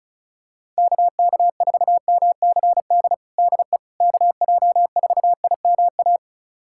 Morse code, heard as a keyed tone, about 34 wpm, 700 Hz. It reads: KK4MCD DE KJ4IMA
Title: USA Callsign DE Callsign - 6 Characters - 35wpm